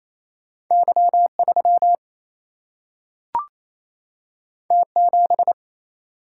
Morse code, heard as a keyed tone, about 28 wpm, 700 Hz. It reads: Y 3 T 7
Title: Single-Letter Single-Number 28wpm